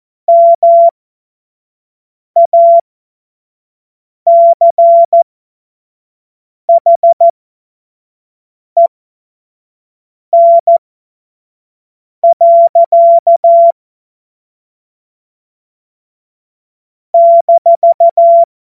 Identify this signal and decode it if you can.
Morse code, heard as a keyed tone, about 14 wpm, spaced as for 5 wpm, 680 Hz. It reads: MACHEN. -
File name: Morsekurs / Witze / Witz 12 März 2024 25 F70